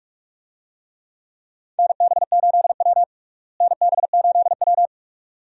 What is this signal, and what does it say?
Morse code, heard as a keyed tone, about 46 wpm, 700 Hz. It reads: NB8W NB8W
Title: USA Call Signs - 4 Characters - Repeated 2x - 45wpm